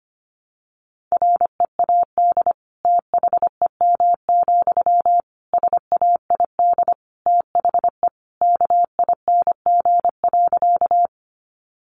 Morse code, heard as a keyed tone, about 25 wpm, 700 Hz. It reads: READ THEM, SAID THE KING.
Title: Sentences from Top 400 Words 25wpm